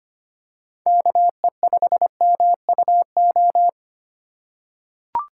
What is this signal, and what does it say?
Morse code, heard as a keyed tone, about 25 wpm, 700 Hz. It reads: KE5MUO
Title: USA Call Signs - 6 Characters - 4x Character Spacing - 25wpm